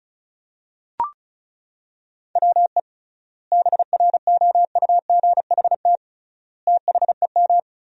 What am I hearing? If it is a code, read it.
Morse code, heard as a keyed tone, about 34 wpm, 700 Hz. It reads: WE BROUGHT THEM
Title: 3x Word-Spacing - Sentences from Top 500 Words - 35wpm